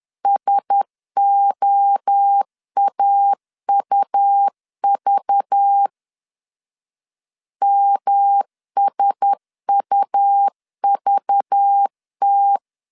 Morse code, heard as a keyed tone, about 10 wpm, 800 Hz. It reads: SOAUV MSUVT
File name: LEZIONI / 6 PKX0 / 6 P 10wpm 2432